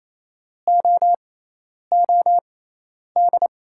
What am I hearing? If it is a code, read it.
Morse code, heard as a keyed tone, about 28 wpm, spaced as for 10 wpm, 700 Hz. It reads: OOD
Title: 6x Character Spacing - Top 500 Words - 28wpm